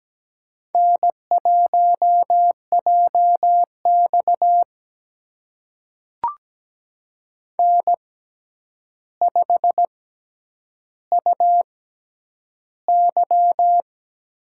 Morse code, heard as a keyed tone, about 17 wpm, spaced as for 9 wpm, 700 Hz. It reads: N1JX N5UY
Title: USA Call Signs - 4 Characters - 6x Character Spacing - 17wpm